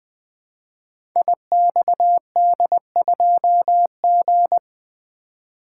Morse code, heard as a keyed tone, about 20 wpm, 700 Hz. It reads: IXD2G
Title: Code Groups - 5 Letter:Numbers - 20wpm